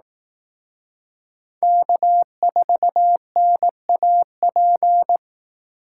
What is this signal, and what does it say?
Morse code, heard as a keyed tone, about 18 wpm, 700 Hz. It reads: K4NAP
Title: Speed Racing - USA 1x3 Call Signs 15wpm